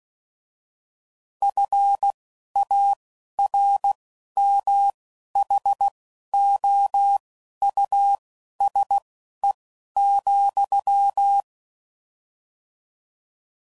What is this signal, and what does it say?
Morse code, heard as a keyed tone, about 16 wpm, spaced as for 11 wpm, 800 Hz. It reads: FARMHOUSE,
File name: Suffixes / General / Houses